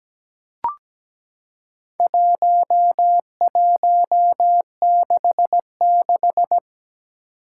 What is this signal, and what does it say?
Morse code, heard as a keyed tone, about 17 wpm, 700 Hz. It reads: 1166
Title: Four Challenging Letter-Number Combinations 17wpm